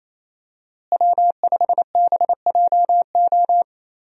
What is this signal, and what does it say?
Morse code, heard as a keyed tone, about 28 wpm, 700 Hz. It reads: W5BJO
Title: USA Call Signs - 5 Characters - Repeated 3x - 28wpm